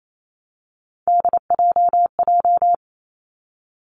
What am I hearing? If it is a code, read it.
Morse code, heard as a keyed tone, about 28 wpm, 690 Hz. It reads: DJJ
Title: Three Challenging Letter-Number Combinations 28wpm